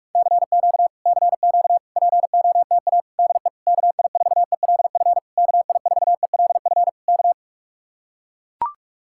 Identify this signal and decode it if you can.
Morse code, heard as a keyed tone, about 44 wpm, 700 Hz. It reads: CQ CQ POTA DE KI4ELF KI4ELF K